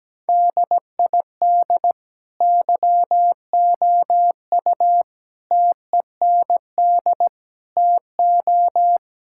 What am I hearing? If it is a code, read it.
Morse code, heard as a keyed tone, about 17 wpm, 700 Hz. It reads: DID YOU TEND TO
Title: Sentences from Top 1000 Words - Review Part 5 - 17wpm